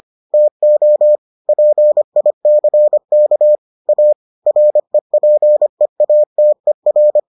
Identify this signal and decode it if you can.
Morse code, heard as a keyed tone, about 25 wpm, 600 Hz. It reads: TO PICK A REPEATER